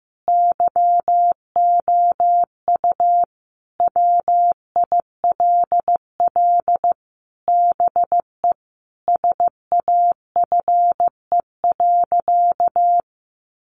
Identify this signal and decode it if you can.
Morse code, heard as a keyed tone, about 15 wpm, 700 Hz. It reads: YOU WILL BE SAFE.
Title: Sentences from Top 1300 Words 15wpm